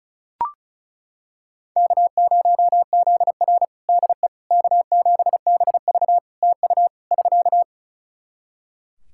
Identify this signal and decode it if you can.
Morse code, heard as a keyed tone, about 34 wpm, 700 Hz. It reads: K0ZR DE K7BV TU <SK>
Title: Practice QSO - Exchange 4 - Ending - 35wpm